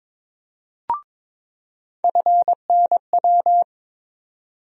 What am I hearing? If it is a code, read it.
Morse code, heard as a keyed tone, about 22 wpm, 700 Hz. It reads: FNW
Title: USA Call Sign Postfixes - 3 Letters 22wpm